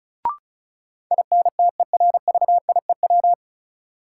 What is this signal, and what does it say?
Morse code, heard as a keyed tone, about 36 wpm, 700 Hz. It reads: INTERVIEW
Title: Sentences from Top 1600 Words - New Word Drill Alt - 35wpm